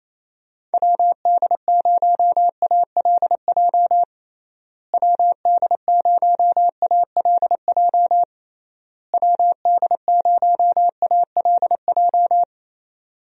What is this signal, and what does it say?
Morse code, heard as a keyed tone, about 28 wpm, 700 Hz. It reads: WD0ALJ WD0ALJ WD0ALJ
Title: USA Call Signs - 6 Characters - Repeated 3x - 28wpm